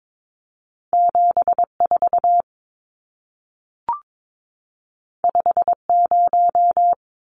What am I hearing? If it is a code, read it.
Morse code, heard as a keyed tone, about 22 wpm, 700 Hz. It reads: 74 50
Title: Numbers - 2-Digit 22wpm